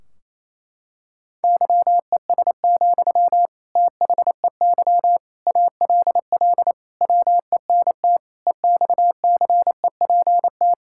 Morse code, heard as a keyed tone, about 30 wpm, 700 Hz. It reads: YES, THEY ALL WENT EXCEPT
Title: Sentences from Top 1600 Words 28wpm